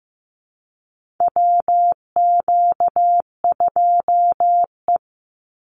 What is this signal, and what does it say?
Morse code, heard as a keyed tone, about 15 wpm, 700 Hz. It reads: WQ2E
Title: Call Signs Repeated 2x - 15wpm